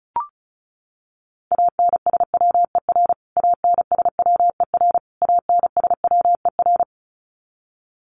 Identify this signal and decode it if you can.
Morse code, heard as a keyed tone, about 34 wpm, 700 Hz. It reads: ANSWER ANSWER ANSWER
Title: Top 300 Words Repeated 35wpm